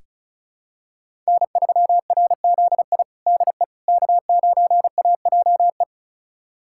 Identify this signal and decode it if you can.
Morse code, heard as a keyed tone, about 34 wpm, 690 Hz. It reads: N3RZI DE K9AJE
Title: USA Callsign DE Callsign - 5 Characters - 35wpm